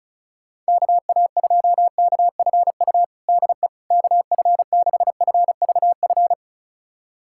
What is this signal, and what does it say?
Morse code, heard as a keyed tone, about 36 wpm, 700 Hz. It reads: KA2KFU DE KF6FVF
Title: USA Callsign DE Callsign - 6 Characters - 35wpm